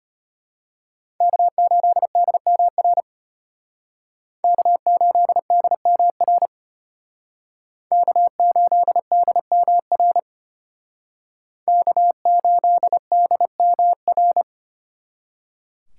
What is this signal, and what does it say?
Morse code, heard as a keyed tone, about 34 wpm, 700 Hz. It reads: K8DMR K8DMR K8DMR K8DMR